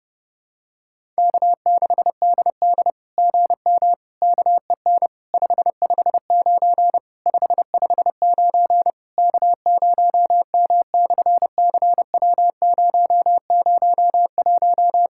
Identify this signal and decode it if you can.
Morse code, heard as a keyed tone, about 28 wpm, 700 Hz. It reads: K6DD GM KEN 559 559 K0M/CW001
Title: SOTA QSO Practice 30wpm